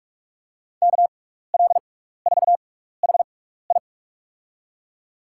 Morse code, heard as a keyed tone, about 45 wpm, spaced as for 15 wpm, 700 Hz. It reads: KL4HI
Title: USA Call Signs - 6x Character Spacing - 45wpm